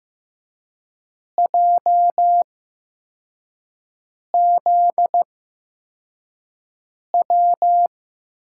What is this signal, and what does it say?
Morse code, heard as a keyed tone, about 15 wpm, spaced as for 4 wpm, 700 Hz. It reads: JZW
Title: USA Call Signs - 6 Characters - 8x Character Spacing - 15wpm